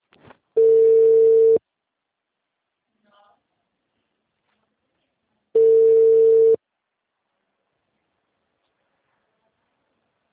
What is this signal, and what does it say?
Ringback tone (China, 450 Hz), with speech.